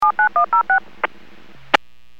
DTMF, heard as telephone touch tones, tones of about 110 ms. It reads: 79106
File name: WPS / 00317 16-46 WPS 451.25 WPS